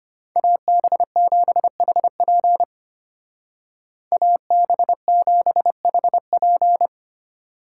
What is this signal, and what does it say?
Morse code, heard as a keyed tone, about 33 wpm, 700 Hz. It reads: AB7HP AB7HP